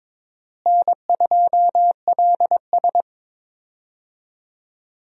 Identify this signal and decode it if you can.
Morse code, heard as a keyed tone, about 22 wpm, 700 Hz. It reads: N2LS
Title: USA Call Signs - 4 Characters - Repeated 2x - 22wpm